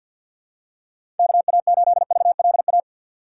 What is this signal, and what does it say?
Morse code, heard as a keyed tone, about 47 wpm, 700 Hz. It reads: KA8VLA
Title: USA Call Signs - 6 Characters - 3x Character Spacing - 50wpm